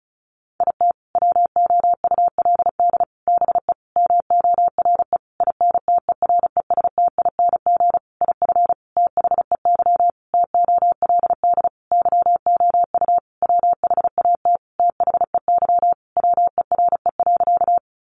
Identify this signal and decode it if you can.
Morse code, heard as a keyed tone, about 34 wpm, 700 Hz. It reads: IT WOULD BE MORE INTERESTING IF THEY TOLD YOU WHAT THEY WERE.
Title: Sentences from Top 1000 Words 35wpm